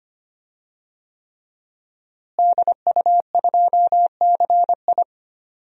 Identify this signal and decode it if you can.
Morse code, heard as a keyed tone, about 25 wpm, 700 Hz. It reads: DU2CI
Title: DXCC-100 Call Signs - 5 Characters - Repeated 3x - 25wpm